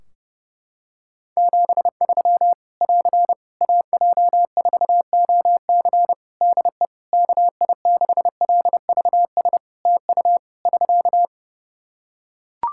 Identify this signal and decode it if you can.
Morse code, heard as a keyed tone, about 30 wpm, 700 Hz. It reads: 73 <AR> AJ4OC DE KI6LVS TU <SK>
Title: Practice QSO - Exchange 4 - Ending - 30wpm